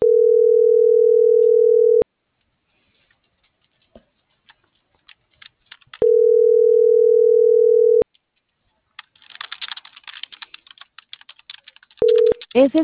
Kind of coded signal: telephone call progress tone

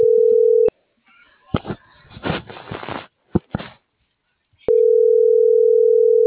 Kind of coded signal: telephone call progress tone

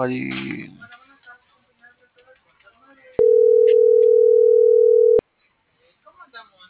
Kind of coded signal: telephone call progress tone

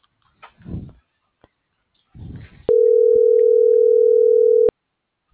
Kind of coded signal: telephone call progress tone

